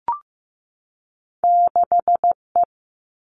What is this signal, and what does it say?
Morse code, heard as a keyed tone, about 15 wpm, 700 Hz. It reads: 6 E
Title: Two Challenging Letter-Number Combinations 15wpm